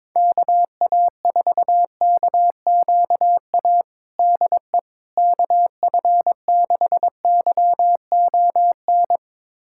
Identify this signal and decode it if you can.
Morse code, heard as a keyed tone, about 22 wpm, 700 Hz. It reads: KA4KQA DE KF6YON
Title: USA Callsign DE Callsign 22wpm